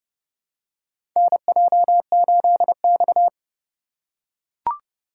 Morse code, heard as a keyed tone, about 30 wpm, 700 Hz. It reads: NJ8X
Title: USA Call Signs - 4 Characters - 2x Character Spacing - 30wpm